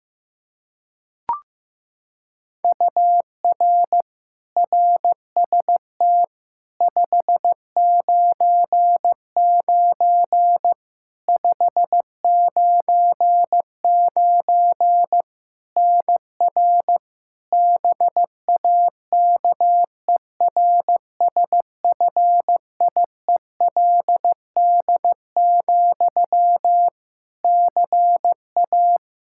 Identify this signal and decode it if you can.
Morse code, heard as a keyed tone, about 15 wpm, 700 Hz. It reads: UR RST 599 599 NR BAKERSFIELD, CA
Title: Practice QSO - Exchange 1 - Beginning - 15wpm